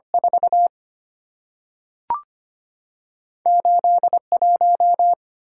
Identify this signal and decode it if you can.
Morse code, heard as a keyed tone, about 25 wpm, 700 Hz. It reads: 4 8 1
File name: Two Letter-Number Combinations - 25wpm